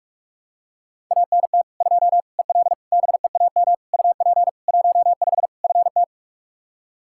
Morse code, heard as a keyed tone, about 42 wpm, 700 Hz. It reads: ANT 2 EL BEAM UP 15 FT
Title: Practice QSO - Exchange 2 - Info - 45wpm